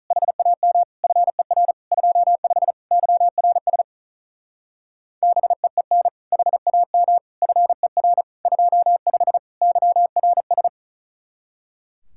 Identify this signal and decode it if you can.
Morse code, heard as a keyed tone, about 43 wpm, 700 Hz. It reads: HAM FER 25 YRS BEEN HAM FER 25 YRS
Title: Speed-Racing Common QSO Phrases Encore - 35wpm